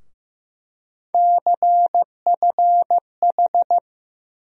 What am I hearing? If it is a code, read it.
Morse code, heard as a keyed tone, about 15 wpm, 700 Hz. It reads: CFH
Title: USA Call Sign Postfixes - 3 Letters 15wpm